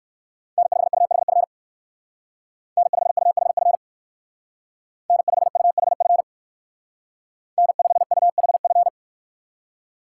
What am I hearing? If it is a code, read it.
Morse code, heard as a keyed tone, about 55 wpm, 700 Hz. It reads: N5UHF N5UHF N5UHF N5UHF
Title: Speed Racing - USA 1x3 Call Signs 45wpm